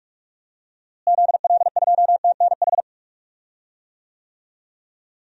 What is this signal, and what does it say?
Morse code, heard as a keyed tone, about 43 wpm, 700 Hz. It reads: ZL2TNH
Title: DXCC-100 Call Signs - 6 Characters - Repeated 2x - 45wpm